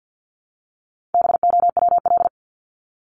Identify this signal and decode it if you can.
Morse code, heard as a keyed tone, about 50 wpm, 700 Hz. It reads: BOWL
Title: Top 1001-1250 Words 50wpm